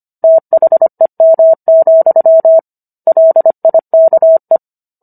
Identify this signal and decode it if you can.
Morse code, heard as a keyed tone, about 25 wpm, 650 Hz. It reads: THEM, LIKE